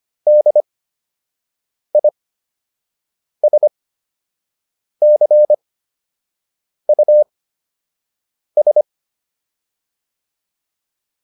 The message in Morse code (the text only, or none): DISCUS